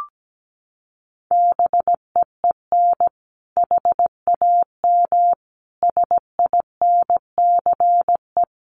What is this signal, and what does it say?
Morse code, heard as a keyed tone, about 17 wpm, 700 Hz. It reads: BEEN HAM SINCE
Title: Common QSO Phrases 17wpm